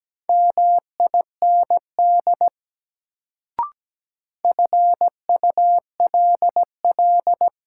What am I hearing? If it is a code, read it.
Morse code, heard as a keyed tone, about 17 wpm, 700 Hz. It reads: MIND FULL
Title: Sentences from Top 500 Words - New Word Drill - 17wpm